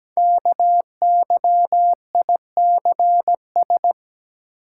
Morse code, heard as a keyed tone, about 17 wpm, 700 Hz. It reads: KYICS
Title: Code Groups - 5 Letters - 17wpm